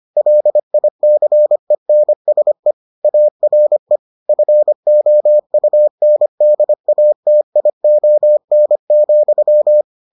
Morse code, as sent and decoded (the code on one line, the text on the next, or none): .-.. .. -.-. -. ... / .- .-. / ..-. --- ..- -. -.. .- - .. --- -. --..--
LICENSE ARE FOUNDATION,